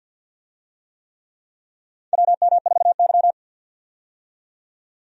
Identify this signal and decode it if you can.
Morse code, heard as a keyed tone, about 52 wpm, 700 Hz. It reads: WM4Y
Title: USA Call Signs - 4 Characters - 50wpm